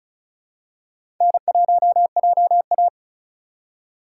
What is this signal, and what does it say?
Morse code, heard as a keyed tone, about 36 wpm, 700 Hz. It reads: N1JA
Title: USA Call Signs - 4 Characters - 2x Character Spacing - 35wpm